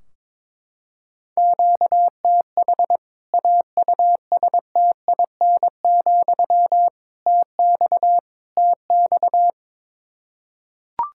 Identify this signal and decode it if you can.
Morse code, heard as a keyed tone, about 22 wpm, 700 Hz. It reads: QTH AUSTIN, TX TX
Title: K1USN SST QSO Practice 22wpm